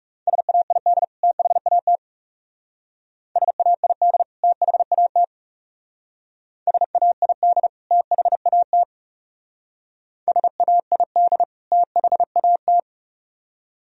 Morse code, heard as a keyed tone, about 37 wpm, 700 Hz. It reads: SAID THAT SAID THAT SAID THAT SAID THAT
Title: Speed-Racing - Sets of 2 Words 30wpm